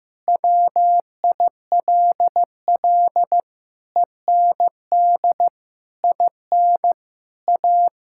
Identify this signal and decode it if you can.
Morse code, heard as a keyed tone, about 15 wpm, 700 Hz. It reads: WILL END IN A
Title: Sentences from Top 1000 Words - Review Part 5 - 15wpm